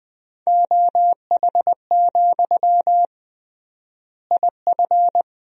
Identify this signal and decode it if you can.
Morse code, heard as a keyed tone, about 19 wpm, 700 Hz. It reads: OH, IF
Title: 3x Word-Spacing - Sentences from Top 500 Words - 20wpm